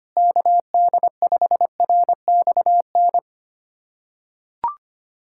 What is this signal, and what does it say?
Morse code, heard as a keyed tone, about 25 wpm, 700 Hz. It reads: KD5RXN
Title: USA Callsign DE Callsign 25wpm